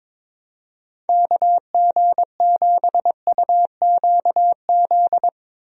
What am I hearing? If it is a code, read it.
Morse code, heard as a keyed tone, about 22 wpm, 700 Hz. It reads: KG7UQZ
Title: USA Call Signs - 6 Characters - Repeated 2x - 22wpm